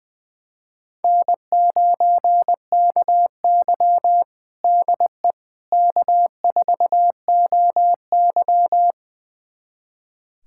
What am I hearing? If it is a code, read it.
Morse code, heard as a keyed tone, about 20 wpm, 700 Hz. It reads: N9KY DE K4OY
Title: USA Callsign DE Callsign - 4 Characters - 20wpm